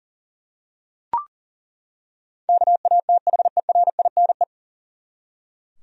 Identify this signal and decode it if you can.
Morse code, heard as a keyed tone, about 39 wpm, 700 Hz. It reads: KATHERINE
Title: USA Common Names - 40wpm